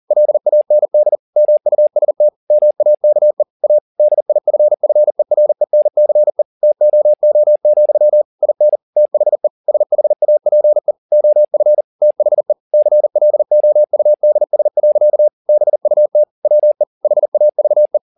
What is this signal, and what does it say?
Morse code, heard as a keyed tone, about 40 wpm, 600 Hz. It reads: LAND MUST MAKE A DIFFERENCE TOO, IN THE SHAPE OF THE CLOUDS. BUT WE HAVE